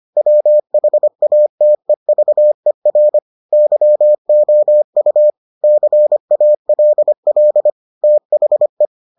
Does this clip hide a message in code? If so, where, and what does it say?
Yes: Morse code from 0.2 to 8.9 s, heard as a keyed tone, about 25 wpm, 600 Hz. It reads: WHATEVER YOU CALL THE